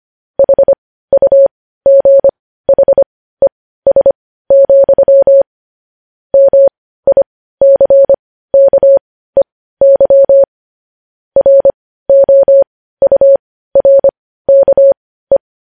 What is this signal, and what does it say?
Morse code, heard as a keyed tone, about 25 wpm, spaced as for 15 wpm, 550 Hz. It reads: HUGHES, MICKEY ROURKE
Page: news-headlines-in-morse-code-at-15-wpm